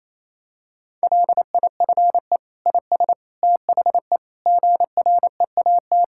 Morse code, heard as a keyed tone, about 28 wpm, 700 Hz. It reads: LIFE IS THE GREAT
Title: Sentences from Top 1000 Words - Review Part 6 - 28wpm